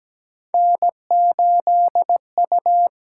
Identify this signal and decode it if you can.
Morse code, heard as a keyed tone, about 17 wpm, 700 Hz. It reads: N8U
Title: USA Call Signs - 5 Characters - 8x Character Spacing - 17wpm